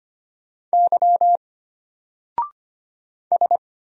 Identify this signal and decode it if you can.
Morse code, heard as a keyed tone, about 25 wpm, 700 Hz. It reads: Y S
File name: Single-Letter-Number-25wpm